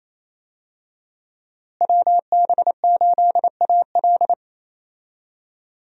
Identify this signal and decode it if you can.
Morse code, heard as a keyed tone, about 28 wpm, 700 Hz. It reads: WB8AL
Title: USA Call Signs - 5 Characters - Repeated 2x - 28wpm